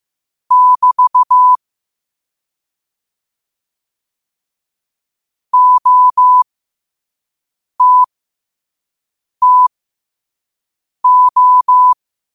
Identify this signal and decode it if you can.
Morse code, heard as a keyed tone, about 15 wpm, spaced as for 5 wpm, 1000 Hz. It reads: <BT> OTTO
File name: qotdfiles / 5wpm